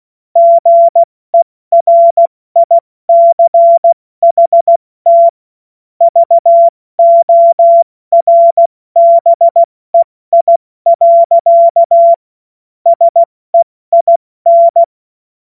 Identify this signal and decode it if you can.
Morse code, heard as a keyed tone, about 16 wpm, 680 Hz. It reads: GERICHT VORBEI. SEIN
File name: Morsekurs / Witze / Witz 20. September 2022 70 F80